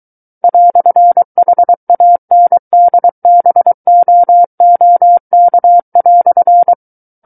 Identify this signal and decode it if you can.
Morse code, heard as a keyed tone, about 23 wpm, 700 Hz. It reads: "HANDBOOK"
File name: Book One / 14-Chapter 14 - 23 WPM